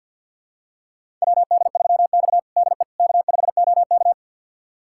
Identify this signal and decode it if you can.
Morse code, heard as a keyed tone, about 52 wpm, 700 Hz. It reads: WD3X DE K5OK
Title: USA Callsign DE Callsign - 4 Characters - 50wpm